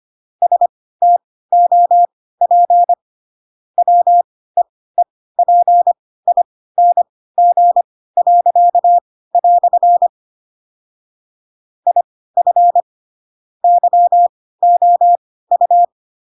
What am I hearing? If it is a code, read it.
Morse code, heard as a keyed tone, about 25 wpm, spaced as for 16 wpm, 700 Hz. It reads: STOP WEEPING." IF YOU
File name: Pinocchio / Pinocchio ch7 10 25 ik1hge